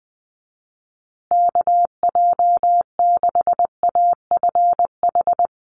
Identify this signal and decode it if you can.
Morse code, heard as a keyed tone, about 20 wpm, 700 Hz. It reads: KJ6AFH
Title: USA Call Signs Repeated 3x 20wpm